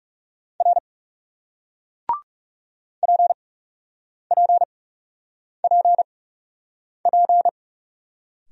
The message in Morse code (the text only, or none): R PPPP